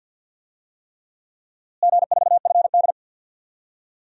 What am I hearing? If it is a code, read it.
Morse code, heard as a keyed tone, about 50 wpm, 700 Hz. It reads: G4FD
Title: DXCC-100 Call Signs - 4 Characters - Repeated 3x - 50wpm